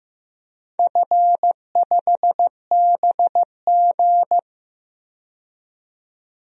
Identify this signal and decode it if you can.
Morse code, heard as a keyed tone, about 15 wpm, 700 Hz. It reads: F5BG
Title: DXCC-100 Call Signs - 4 Characters - 15wpm